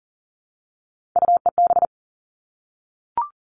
Morse code, heard as a keyed tone, about 40 wpm, 700 Hz. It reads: UEB
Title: All Cut Numbers - 3-Digit 40wpm